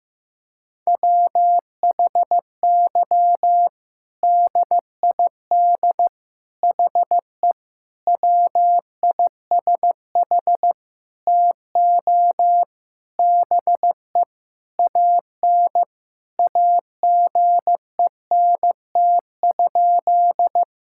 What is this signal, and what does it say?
Morse code, heard as a keyed tone, about 15 wpm, 700 Hz. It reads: WHY DID HE WISH TO BE AN AGENT?
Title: Sentences from Top 1600 Words 15wpm